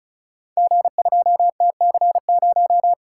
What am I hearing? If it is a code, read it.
Morse code, heard as a keyed tone, about 36 wpm, 700 Hz. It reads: G2TC0
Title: Code Groups - 5 Letter:Numbers - 35wpm